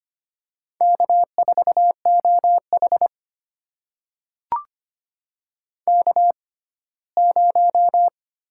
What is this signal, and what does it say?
Morse code, heard as a keyed tone, about 25 wpm, spaced as for 17 wpm, 700 Hz. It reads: K4OH K0
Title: USA Call Signs - 4 Characters - 6x Character Spacing - 25wpm